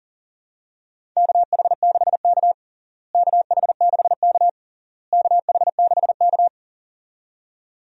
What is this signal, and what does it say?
Morse code, heard as a keyed tone, about 40 wpm, 700 Hz. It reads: KH6K KH6K KH6K
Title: DXCC-100 Call Signs - 4 Characters - Repeated 3x - 40wpm